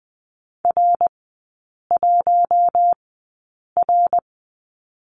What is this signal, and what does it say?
Morse code, heard as a keyed tone, about 20 wpm, spaced as for 8 wpm, 700 Hz. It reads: R1R
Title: 093 Lesson 16e - Letter c - Three Characters 20wpm